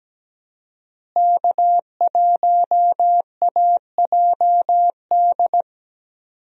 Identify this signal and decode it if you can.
Morse code, heard as a keyed tone, about 17 wpm, 700 Hz. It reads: K1AJD
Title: USA Call Signs - 5 Characters - Repeated 2x - 17wpm